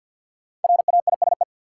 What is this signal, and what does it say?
Morse code, heard as a keyed tone, about 46 wpm, 700 Hz. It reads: RAISE
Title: Sentences from Top 1400 Words - New Word Drill Alt - 50wpm